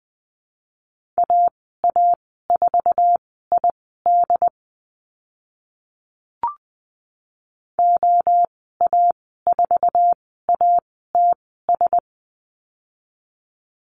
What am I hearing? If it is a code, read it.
Morse code, heard as a keyed tone, about 20 wpm, spaced as for 14 wpm, 700 Hz. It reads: AA4ID OA4ATS
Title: 069 Lesson 12g - Letter d - Call Signs 20wpm